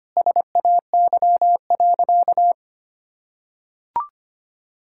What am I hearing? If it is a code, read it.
Morse code, heard as a keyed tone, about 25 wpm, 700 Hz. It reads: SAY.
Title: Sentences from Top 1600 Words 25wpm